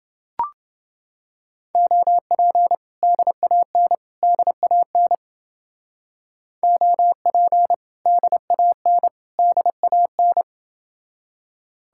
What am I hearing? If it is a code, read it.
Morse code, heard as a keyed tone, about 30 wpm, 700 Hz. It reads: OP DAN DAN OP DAN DAN